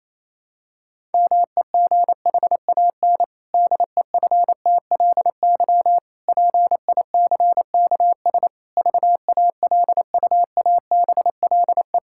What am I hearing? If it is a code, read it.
Morse code, heard as a keyed tone, about 28 wpm, 700 Hz. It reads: MEGHAN DEFTLY PICKS VALUABLE
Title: Pangram Challenge v2 - 28wpm